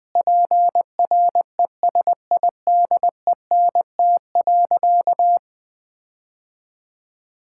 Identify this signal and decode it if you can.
Morse code, heard as a keyed tone, about 20 wpm, 700 Hz. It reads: PRESIDENT.
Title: Sentences from Top 1600 Words 20wpm